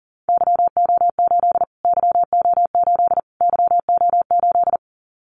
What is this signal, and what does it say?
Morse code, heard as a keyed tone, about 40 wpm, 700 Hz. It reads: YO8 YO8 YO8
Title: Callsign Prefixes for Top 25 DXCC - Repeated 3x - 40wpm